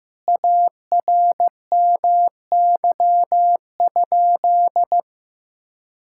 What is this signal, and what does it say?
Morse code, heard as a keyed tone, about 15 wpm, 700 Hz. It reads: ARMY?
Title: Sentences from Top 1000 Words - Review Part 5 - 15wpm